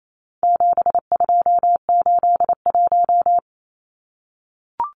Morse code, heard as a keyed tone, about 28 wpm, 700 Hz. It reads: 7281
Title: Numbers - 4-Digit 28wpm